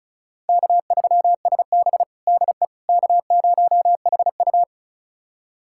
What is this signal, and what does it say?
Morse code, heard as a keyed tone, about 36 wpm, 700 Hz. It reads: K3SB DE K0HU
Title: USA Callsign DE Callsign - 4 Characters - 35wpm